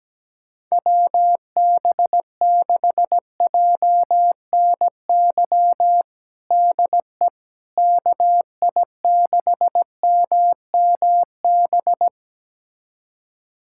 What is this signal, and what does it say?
Morse code, heard as a keyed tone, about 17 wpm, 700 Hz. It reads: WB6JNY DE KI6MMB
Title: USA Callsign DE Callsign 17wpm